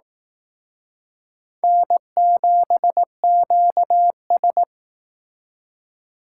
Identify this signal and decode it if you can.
Morse code, heard as a keyed tone, about 18 wpm, 700 Hz. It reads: N7QS
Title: Speed Racing - USA 1x2 Call Signs 15wpm